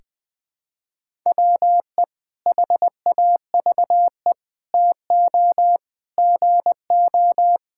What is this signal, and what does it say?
Morse code, heard as a keyed tone, about 19 wpm, 700 Hz. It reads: WE HAVE TO GO
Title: Sentences with 3-Word Phrases - 20wpm